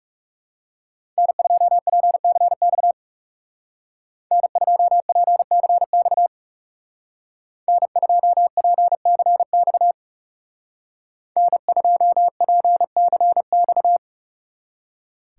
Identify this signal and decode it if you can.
Morse code, heard as a keyed tone, about 39 wpm, 700 Hz. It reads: N2PCX N2PCX N2PCX N2PCX